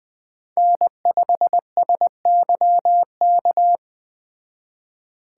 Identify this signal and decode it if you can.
Morse code, heard as a keyed tone, about 21 wpm, 700 Hz. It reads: N5SYK
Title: Speed Racing - USA 1x3 Call Signs 17wpm